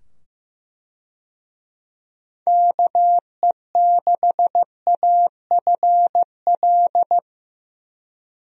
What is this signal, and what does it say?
Morse code, heard as a keyed tone, about 15 wpm, 700 Hz. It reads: KE6AFL
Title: USA Call Signs - 6 Characters - 15wpm